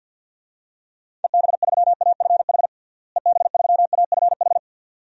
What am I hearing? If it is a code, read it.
Morse code, heard as a keyed tone, about 49 wpm, 700 Hz. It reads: EB3AFH EB3AFH
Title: DXCC-100 Call Signs - 6 Characters - Repeated 2x - 50wpm